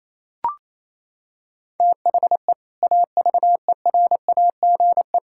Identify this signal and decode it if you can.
Morse code, heard as a keyed tone, about 27 wpm, 700 Hz. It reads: THE AVERAGE
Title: Sentences from Top 1000 Words - Review Part 6 - 28wpm